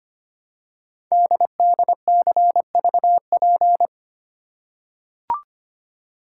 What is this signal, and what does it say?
Morse code, heard as a keyed tone, about 25 wpm, 700 Hz. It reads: DDCVP
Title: Code Groups - 5 Letters - 25wpm